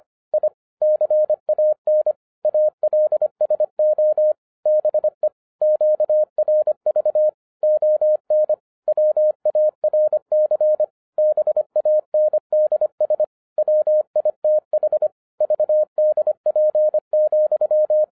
Morse code, heard as a keyed tone, about 25 wpm, 610 Hz. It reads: I CAN ALSO BE QRV ON WARC BANDS WITH VDP,